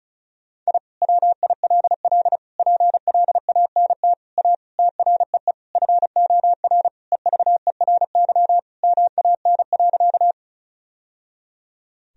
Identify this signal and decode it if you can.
Morse code, heard as a keyed tone, about 34 wpm, 700 Hz. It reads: I WILL PLANT A TREE FOR EVERY MAN.
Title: Sentences from Top 1000 Words - Review Part 1 - 35wpm